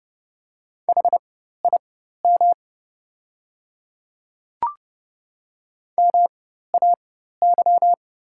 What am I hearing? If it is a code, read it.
Morse code, heard as a keyed tone, about 30 wpm, spaced as for 14 wpm, 700 Hz. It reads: HIM MAY
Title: 4x Character Spacing - Top 100 Words - 30wpm